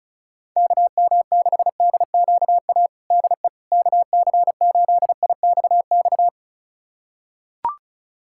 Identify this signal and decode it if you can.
Morse code, heard as a keyed tone, about 36 wpm, 700 Hz. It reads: KM6DQA DE KC8IXX
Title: USA Callsign DE Callsign 35wpm